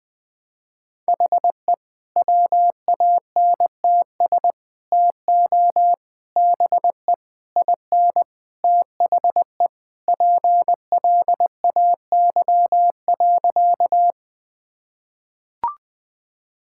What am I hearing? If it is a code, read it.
Morse code, heard as a keyed tone, about 20 wpm, 700 Hz. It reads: HE WANTS TO BE IN THE PLAY.
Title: Sentences with 4-Word Phrases - Spoken Beginning - 20wpm